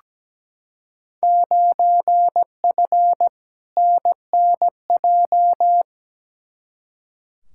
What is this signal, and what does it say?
Morse code, heard as a keyed tone, about 17 wpm, 700 Hz. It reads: 9F NNJ
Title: Field Day Exchanges Encore - 17wpm